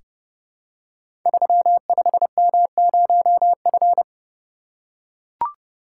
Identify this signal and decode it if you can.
Morse code, heard as a keyed tone, about 28 wpm, 700 Hz. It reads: 35M0F
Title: Code Groups - 5 Letter:Numbers - 30wpm